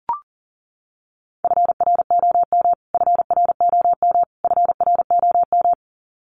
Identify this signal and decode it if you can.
Morse code, heard as a keyed tone, about 40 wpm, 700 Hz. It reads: FROM FROM FROM
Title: Top 200 Words Repeated 40wpm